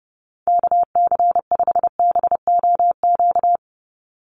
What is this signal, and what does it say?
Morse code, heard as a keyed tone, about 30 wpm, 700 Hz. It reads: KC5BOQ
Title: USA Call Signs 30wpm